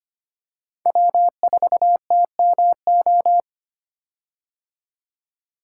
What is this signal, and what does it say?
Morse code, heard as a keyed tone, about 25 wpm, 700 Hz. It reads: W4TMO
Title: K1USN SST QSO Practice - 25wpm